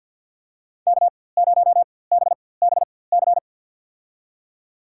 Morse code, heard as a keyed tone, about 53 wpm, spaced as for 24 wpm, 700 Hz. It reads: K0BBC
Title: USA Call Signs - 5 Characters - 4x Character Spacing - 50wpm